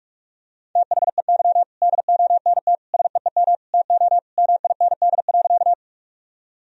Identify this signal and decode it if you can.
Morse code, heard as a keyed tone, about 46 wpm, 700 Hz. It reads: THEY DONT SEEM TO MIND.
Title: Sentences from Top 1000 Words - Review Part 1 - 45wpm